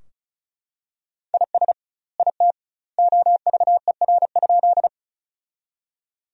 Morse code, heard as a keyed tone, about 36 wpm, 700 Hz. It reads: IS IT OVER?
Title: 2x Word-Spacing - Sentences from Top 100 Words - 35wpm